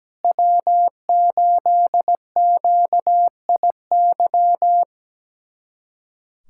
Morse code, heard as a keyed tone, about 17 wpm, 700 Hz. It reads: W8QIY
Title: USA Callsign DE Callsign - 5 Characters - 17wpm